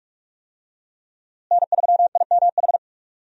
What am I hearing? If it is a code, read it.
Morse code, heard as a keyed tone, about 45 wpm, 700 Hz. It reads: N3IMH
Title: USA Call Signs - 5 Characters - Repeated 2x - 45wpm